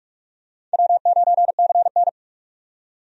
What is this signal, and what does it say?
Morse code, heard as a keyed tone, about 46 wpm, 700 Hz. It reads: W9CN
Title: USA Call Signs - 4 Characters - 45wpm